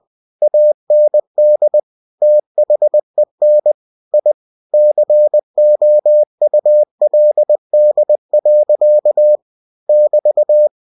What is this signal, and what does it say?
Morse code, heard as a keyed tone, about 20 wpm, 600 Hz. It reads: AND THEN I COULD. <BT>
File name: cw-books / winnie-the-pooh / winnie-20wpm-page-0052